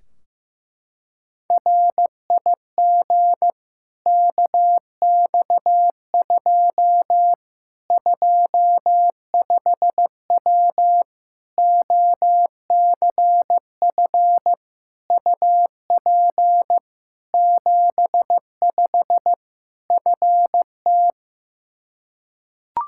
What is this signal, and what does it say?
Morse code, heard as a keyed tone, about 15 wpm, 700 Hz. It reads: RIG KX2 25W OCF UP 75 FT
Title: K1USN SST QSO Practice 15wpm